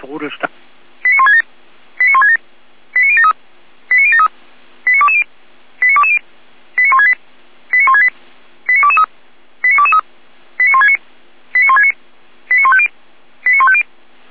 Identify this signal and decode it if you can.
Five-tone selective call (ZVEI), heard as quick runs of five tones, tones of about 70 ms, with speech. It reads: 89167 89167 89073 89073 89220 89220 89168 89168 89223 89223 89169 89169 89150 89150